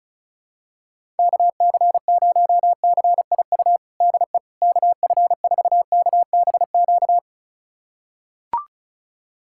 Morse code, heard as a keyed tone, about 36 wpm, 700 Hz. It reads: KC0CIU DE KF4KBQ
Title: USA Callsign DE Callsign - 6 Characters - 35wpm